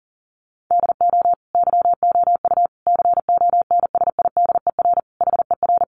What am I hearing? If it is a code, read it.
Morse code, heard as a keyed tone, about 40 wpm, 700 Hz. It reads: DO YOU CONSIDER HER
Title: Sentences from Top 600 Words 40wpm